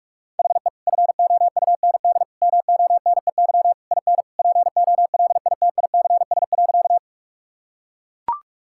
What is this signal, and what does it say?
Morse code, heard as a keyed tone, about 42 wpm, 700 Hz. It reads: HE FOUND MONEY IN POLITICS.
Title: Sentences from Top 1000 Words - Review Part 6 - 45wpm